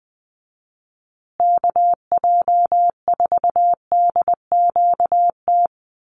Morse code, heard as a keyed tone, about 20 wpm, 700 Hz. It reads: KJ4DQT
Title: USA Call Signs Repeated 3x 20wpm